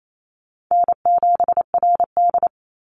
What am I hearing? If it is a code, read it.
Morse code, heard as a keyed tone, about 28 wpm, 700 Hz. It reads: N7RD
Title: Call Signs - 28wpm